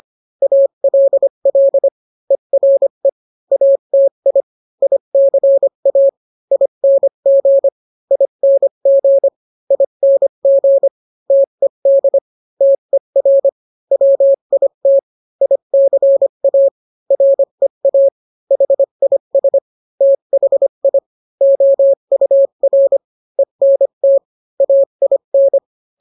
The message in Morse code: ALL ERE ATI ICA ING ING ING TED TER WIT ICA REA HIS THI OUR ENT AIN